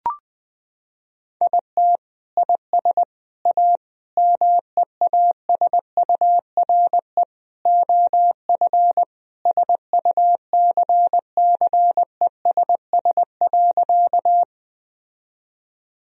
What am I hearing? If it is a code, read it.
Morse code, heard as a keyed tone, about 20 wpm, 700 Hz. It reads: IT IS A MEASURE OF SUCCESS.
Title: Sentences from Top 1000 Words - Review Part 2 - 20wpm